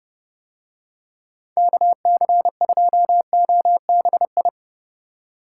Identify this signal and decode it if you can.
Morse code, heard as a keyed tone, about 27 wpm, 700 Hz. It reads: KC2OBI